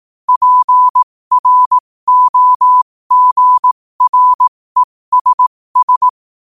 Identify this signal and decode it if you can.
Morse code, heard as a keyed tone, about 18 wpm, spaced as for 15 wpm, 1000 Hz. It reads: PROGRESS